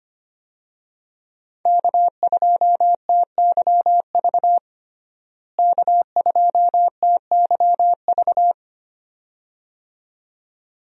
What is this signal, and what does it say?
Morse code, heard as a keyed tone, about 25 wpm, 700 Hz. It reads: K2TYV K2TYV
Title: USA Call Signs - 5 Characters - Repeated 2x - 25wpm